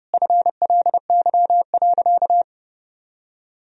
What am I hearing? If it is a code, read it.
Morse code, heard as a keyed tone, about 30 wpm, 700 Hz. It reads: FLY.
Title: Sentences from Top 1000 Words - Review Part 2 - 30wpm